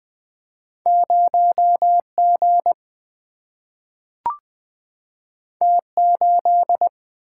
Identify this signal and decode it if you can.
Morse code, heard as a keyed tone, about 20 wpm, 700 Hz. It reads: 0 G T 8
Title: Single-Letter Single-Number 20wpm